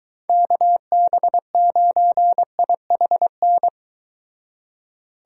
Morse code, heard as a keyed tone, about 23 wpm, 700 Hz. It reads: KB9IHN